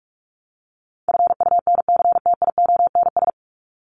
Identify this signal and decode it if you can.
Morse code, heard as a keyed tone, about 44 wpm, 700 Hz. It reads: FUNCTIONS
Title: ES:S Suffix - Top 100 Words with Base Words 45wpm